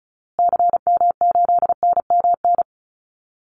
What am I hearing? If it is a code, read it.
Morse code, heard as a keyed tone, about 34 wpm, 700 Hz. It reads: CM8NMN
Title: Call Signs Repeated 2x - 35wpm